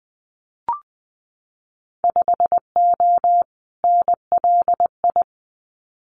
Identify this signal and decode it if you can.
Morse code, heard as a keyed tone, about 20 wpm, 700 Hz. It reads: 5O NLI
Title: Winter Field Day Exchanges 20wpm